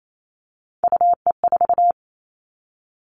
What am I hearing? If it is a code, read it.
Morse code, heard as a keyed tone, about 28 wpm, 700 Hz. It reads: UE4
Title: Three Challenging Letter-Number Combinations 28wpm